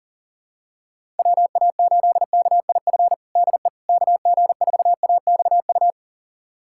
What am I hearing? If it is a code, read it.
Morse code, heard as a keyed tone, about 37 wpm, 700 Hz. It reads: WA8KIF DE KG4AXU
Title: USA Callsign DE Callsign - 6 Characters - 40wpm